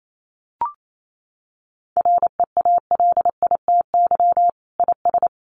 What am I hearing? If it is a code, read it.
Morse code, heard as a keyed tone, about 28 wpm, 700 Hz. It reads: REALITY IS
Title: Sentences from Top 1200 Words 28wpm